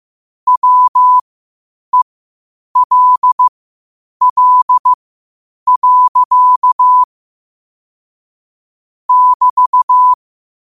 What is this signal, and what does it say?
Morse code, heard as a keyed tone, about 15 wpm, spaced as for 8 wpm, 1000 Hz. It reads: WELL. <BT>
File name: qotdfiles / 8wpm